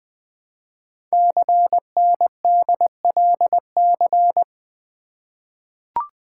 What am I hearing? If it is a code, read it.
Morse code, heard as a keyed tone, about 20 wpm, 700 Hz. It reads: CNDLC
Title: Code Groups - 5 Letters - 20wpm